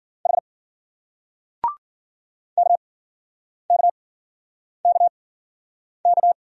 Morse code, heard as a keyed tone, about 51 wpm, spaced as for 9 wpm, 700 Hz. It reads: H KKKK